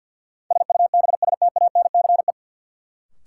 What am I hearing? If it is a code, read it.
Morse code, heard as a keyed tone, about 49 wpm, 700 Hz. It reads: SUBSTANCE
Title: Sentences from Top 1500 Words - New Word Drill Alt - 50wpm